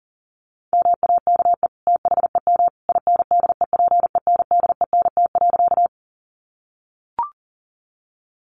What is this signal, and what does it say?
Morse code, heard as a keyed tone, about 40 wpm, 700 Hz. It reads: MAKE THEM INDEPENDENT.
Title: Sentences from Top 900 Words 40wpm